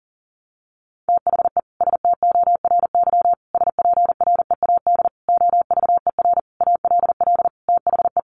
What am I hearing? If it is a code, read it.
Morse code, heard as a keyed tone, about 40 wpm, 700 Hz. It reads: THE STORY SPREAD OVER ALL THE
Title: Sentences from Top 1200 Words 40wpm